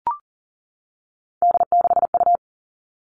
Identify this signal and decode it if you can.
Morse code, heard as a keyed tone, about 40 wpm, 700 Hz. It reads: D6U
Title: All Cut Numbers - 3-Digit 40wpm